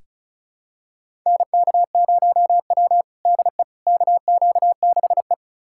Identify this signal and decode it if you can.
Morse code, heard as a keyed tone, about 36 wpm, 700 Hz. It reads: NK0W DE KQ6E
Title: USA Callsign DE Callsign - 4 Characters - 35wpm